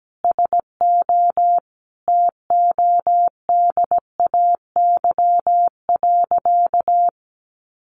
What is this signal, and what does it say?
Morse code, heard as a keyed tone, about 17 wpm, 700 Hz. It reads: SO TODAY.
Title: Sentences from Top 1500 Words 17wpm